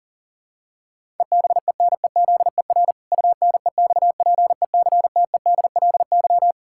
Morse code, heard as a keyed tone, about 40 wpm, 700 Hz. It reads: EBENEZER UNEXPECTEDLY
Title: Pangram Challenge v2 - 40wpm